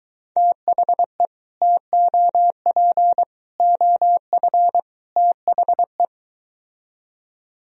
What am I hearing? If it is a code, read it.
Morse code, heard as a keyed tone, about 23 wpm, 700 Hz. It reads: THE TOP OF THE